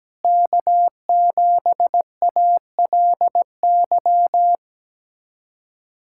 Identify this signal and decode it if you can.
Morse code, heard as a keyed tone, about 17 wpm, 700 Hz. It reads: K7ALY
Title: USA Callsign DE Callsign - 5 Characters - 17wpm